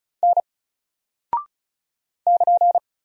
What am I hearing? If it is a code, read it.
Morse code, heard as a keyed tone, about 36 wpm, 700 Hz. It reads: N <KN>
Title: Single Character (Letter:Number:Punctuation:Prosign) 35wpm